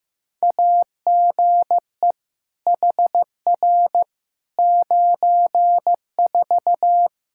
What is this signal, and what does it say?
Morse code, heard as a keyed tone, about 15 wpm, 700 Hz. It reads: AGE HR 94
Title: Practice QSO - Exchange 3 - More Info - 15wpm